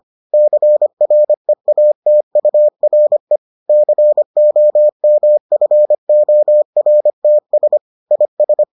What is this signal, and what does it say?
Morse code, heard as a keyed tone, about 25 wpm, 600 Hz. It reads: CREATURE COMFORTS IS